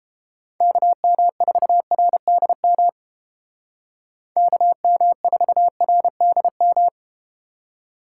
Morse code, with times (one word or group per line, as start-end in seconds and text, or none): 0.6-2.9 KM4RDM
4.4-6.9 KM4RDM